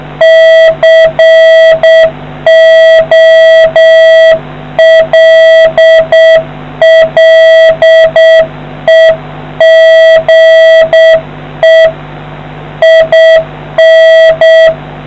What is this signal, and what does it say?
Morse code, heard as a keyed tone, about 7 wpm, 650 Hz. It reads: COLLEGE IN